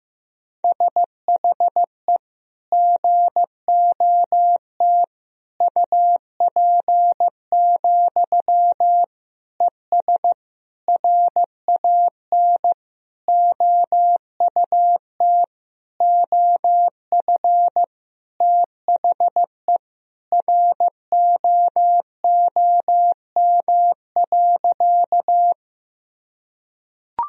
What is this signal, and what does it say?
Morse code, heard as a keyed tone, about 15 wpm, 700 Hz. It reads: SHE GOT UP, ES RAN OUT OF THE ROOM.
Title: Sentences from Top 500 Words and Abbreviations 15wpm